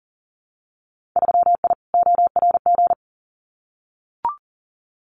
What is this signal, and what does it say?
Morse code, heard as a keyed tone, about 40 wpm, 700 Hz. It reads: 3I ORG
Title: Winter Field Day Exchanges Encore 40wpm